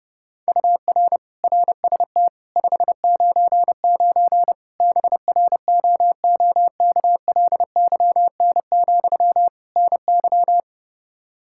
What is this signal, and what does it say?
Morse code, heard as a keyed tone, about 30 wpm, 700 Hz. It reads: UR RST 599 BROOKLYN, NY
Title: Practice QSO - Exchange 1 - Beginning - 30wpm